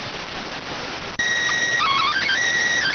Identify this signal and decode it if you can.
SSTV (slow-scan television), PD 290.